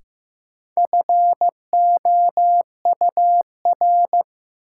Morse code, heard as a keyed tone, about 15 wpm, 700 Hz. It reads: FOUR
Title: Sentences from Top 300 Words - New Word Drill - 15wpm